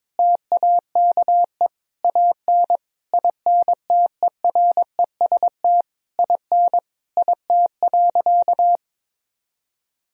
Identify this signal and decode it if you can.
Morse code, heard as a keyed tone, about 22 wpm, 700 Hz. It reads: TAKE AN INTEREST IN IT.
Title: Sentences from Top 1000 Words - Review Part 2 - 22wpm